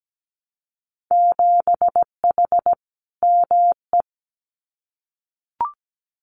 Morse code, heard as a keyed tone, about 17 wpm, 700 Hz. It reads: 7H ME
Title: Winter Field Day Exchanges Encore 17wpm